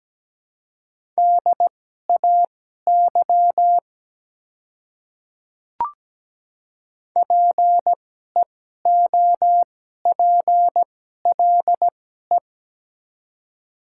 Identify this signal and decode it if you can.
Morse code, heard as a keyed tone, about 17 wpm, spaced as for 12 wpm, 700 Hz. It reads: DAY PEOPLE
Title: 2x Character Spacing - Top 100 Words - 17wpm